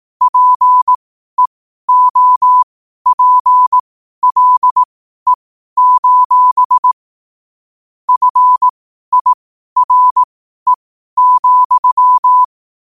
Morse code, heard as a keyed tone, about 18 wpm, spaced as for 12 wpm, 1000 Hz. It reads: PEOPLE: FIRE,
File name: 12 wpm